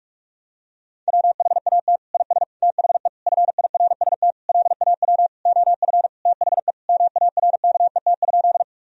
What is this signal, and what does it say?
Morse code, heard as a keyed tone, about 46 wpm, 700 Hz. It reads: WHAT IS THE FIRST LAW OF THE MARKET?
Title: Sentences from Top 1000 Words - Review Part 3 - 45wpm